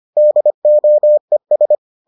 Morse code, heard as a keyed tone, about 25 wpm, 600 Hz. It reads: DOES